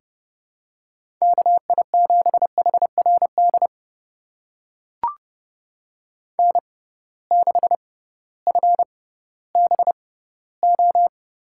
Morse code, heard as a keyed tone, about 31 wpm, spaced as for 16 wpm, 700 Hz. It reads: KI7HRD N6FBO
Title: USA Call Signs - 6x Character Spacing - 30wpm